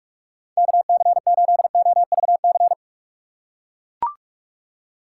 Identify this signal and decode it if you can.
Morse code, heard as a keyed tone, about 43 wpm, 700 Hz. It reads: KC8OVC
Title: USA Call Signs - 6 Characters - 8x Character Spacing - 45wpm